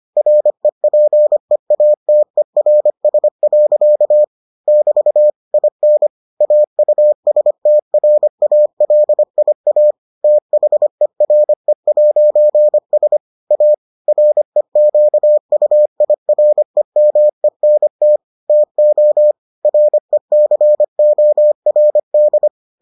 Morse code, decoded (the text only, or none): REPEATERS. <BT> IN AUSTRALIA THERE'S A REQUIREMENT TO RECORD